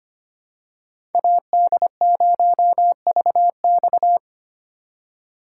Morse code, heard as a keyed tone, about 25 wpm, 700 Hz. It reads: AD0VX
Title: USA Call Signs - 5 Characters - 3x Character Spacing - 25wpm